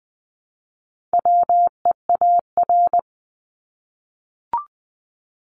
Morse code, heard as a keyed tone, about 20 wpm, 700 Hz. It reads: WEAR
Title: Top 1251-1500 Words 20wpm